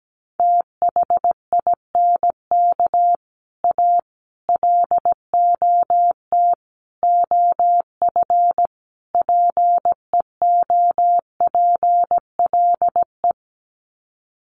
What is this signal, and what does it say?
Morse code, heard as a keyed tone, about 17 wpm, 700 Hz. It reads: THINK A LOT OF PEOPLE
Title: Sets of 5 Words 17wpm